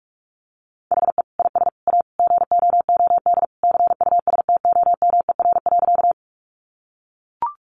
Morse code, heard as a keyed tone, about 44 wpm, 700 Hz. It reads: HE IS A GOOD CUSTOMER.
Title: Sentences from Top 1100 Words 45wpm